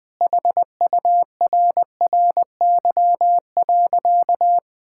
Morse code, heard as a keyed tone, about 20 wpm, 700 Hz. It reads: HURRY.
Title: Sentences with 5-Word Phrases - Spoken Beginning - 20wpm